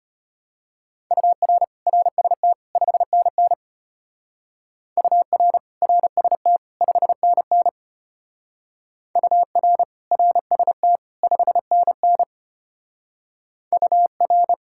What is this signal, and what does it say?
Morse code, heard as a keyed tone, about 30 wpm, 700 Hz. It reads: UR RST 5NN UR RST 5NN UR RST 5NN UR
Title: Speed-Racing Common QSO Phrases - 25wpm